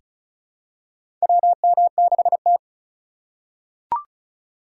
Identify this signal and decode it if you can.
Morse code, heard as a keyed tone, about 34 wpm, 700 Hz. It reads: WM6T
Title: USA Call Signs - 4 Characters - 4x Character Spacing - 35wpm